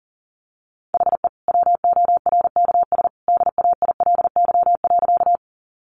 Morse code, heard as a keyed tone, about 40 wpm, 700 Hz. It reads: HE WORKS DAILY.
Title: Sentences from Top 1100 Words 40wpm